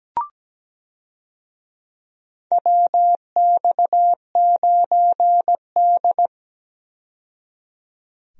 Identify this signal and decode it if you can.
Morse code, heard as a keyed tone, about 17 wpm, 700 Hz. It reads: WX9D
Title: USA Call Signs - 4 Characters - 17wpm